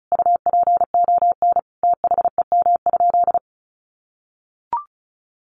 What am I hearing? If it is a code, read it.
Morse code, heard as a keyed tone, about 34 wpm, 700 Hz. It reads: UPON THEM?
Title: Sentences from Top 1300 Words 35wpm